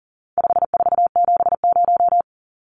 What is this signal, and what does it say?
Morse code, heard as a keyed tone, about 40 wpm, 700 Hz. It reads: 5470
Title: Numbers - 4-Digit 40wpm